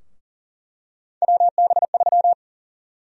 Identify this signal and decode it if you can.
Morse code, heard as a keyed tone, about 40 wpm, 700 Hz. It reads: WB3
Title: USA Call Sign Prefixes - Repeated 2x - 40wpm